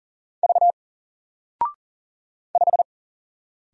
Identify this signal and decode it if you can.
Morse code, heard as a keyed tone, about 40 wpm, 700 Hz. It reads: V 5
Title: Single Letter-Number 40wpm